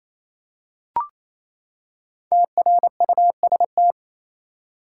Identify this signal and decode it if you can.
Morse code, heard as a keyed tone, about 28 wpm, 700 Hz. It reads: TRUST
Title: Five Letter Words - 28wpm